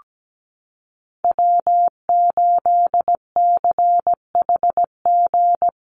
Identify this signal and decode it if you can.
Morse code, heard as a keyed tone, about 17 wpm, 700 Hz. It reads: W8CHG
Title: USA Call Signs Repeated 3x 17wpm